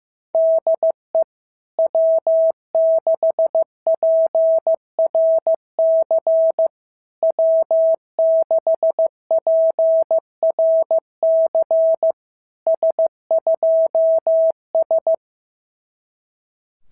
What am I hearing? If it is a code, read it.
Morse code, heard as a keyed tone, about 15 wpm, 650 Hz. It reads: DE W6PRC W6PRC S2S
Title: SOTA QSO Practice 15wpm